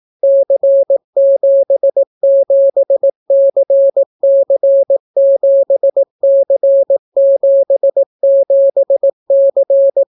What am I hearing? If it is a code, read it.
Morse code, heard as a keyed tone, about 18 wpm, 550 Hz. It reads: C77CC7C77C